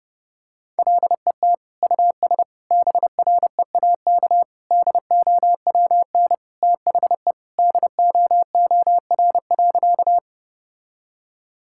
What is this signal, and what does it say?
Morse code, heard as a keyed tone, about 30 wpm, 700 Hz. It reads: LET US BREAK DOWN THE DOOR.
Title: Sentences from Top 1000 Words - Review Part 6 - 30wpm